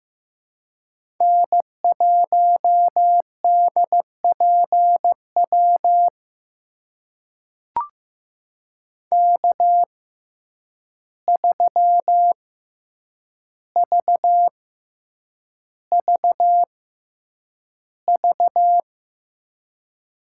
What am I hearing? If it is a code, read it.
Morse code, heard as a keyed tone, about 15 wpm, spaced as for 8 wpm, 700 Hz. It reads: N1DPW K3VVV
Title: USA Call Signs - 5 Characters - 6x Character Spacing - 15wpm